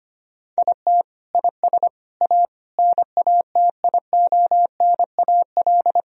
Morse code, heard as a keyed tone, about 25 wpm, 700 Hz. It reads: IT IS A NATIONAL
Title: Sentences from Top 1000 Words - Review Part 4 - 25wpm